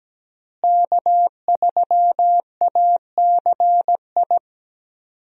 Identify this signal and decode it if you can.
Morse code, heard as a keyed tone, about 17 wpm, 700 Hz. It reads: K3ACI
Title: USA Call Signs - 5 Characters - Repeated 2x - 17wpm